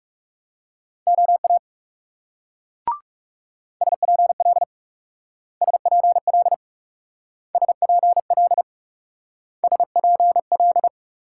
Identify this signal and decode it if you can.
Morse code, heard as a keyed tone, about 38 wpm, 700 Hz. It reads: OA SPL SPL SPL SPL